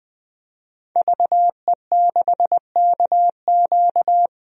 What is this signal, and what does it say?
Morse code, heard as a keyed tone, about 21 wpm, 700 Hz. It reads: VE6KQ
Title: DXCC-100 Call Signs - 5 Characters - 20wpm